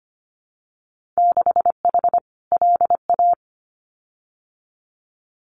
Morse code, heard as a keyed tone, about 25 wpm, 700 Hz. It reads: 6H LA
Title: Winter Field Day Exchanges Encore 25wpm